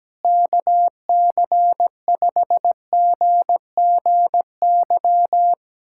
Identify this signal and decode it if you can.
Morse code, heard as a keyed tone, about 17 wpm, 700 Hz. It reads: KC5GGY
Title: USA Callsign DE Callsign - 6 Characters - 17wpm